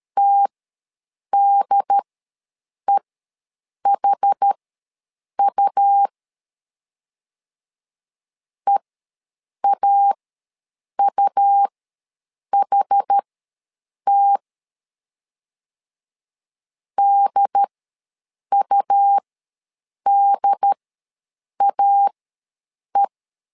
Morse code, heard as a keyed tone, about 13 wpm, spaced as for 7 wpm, 800 Hz. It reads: TDEHU EAUHT DUDAE
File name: LEZIONI / 2 UHDN / 2 UHD ws 12wpm 1722